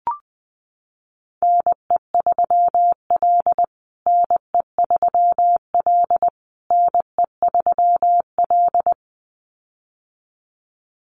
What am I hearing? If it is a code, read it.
Morse code, heard as a keyed tone, about 20 wpm, 700 Hz. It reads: NE3L NE3L NE3L
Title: USA Call Signs Repeated 3x 20wpm